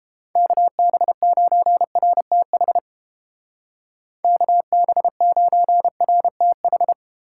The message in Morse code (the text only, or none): KB9RTH KB9RTH